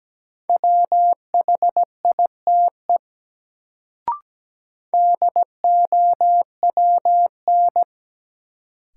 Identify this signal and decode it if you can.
Morse code, heard as a keyed tone, about 17 wpm, 700 Hz. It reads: WHITE DOWN
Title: Sentences from Top 300 Words - New Word Drill - 17wpm